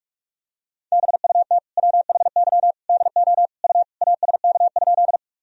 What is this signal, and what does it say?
Morse code, heard as a keyed tone, about 41 wpm, 700 Hz. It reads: BUT WHY DO U ASK?
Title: Sentences from Top 200 Words and Abbreviations 45wpm